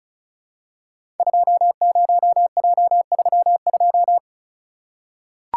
Morse code, heard as a keyed tone, about 36 wpm, 690 Hz. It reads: 20J32
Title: Code Groups - 5 Letter:Numbers - 35wpm